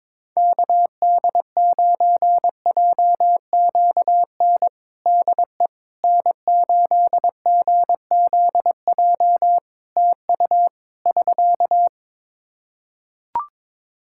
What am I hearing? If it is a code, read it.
Morse code, heard as a keyed tone, about 22 wpm, 700 Hz. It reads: KD9JQN DE N8GZJ TU <SK>
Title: K1USN SST QSO Practice 22wpm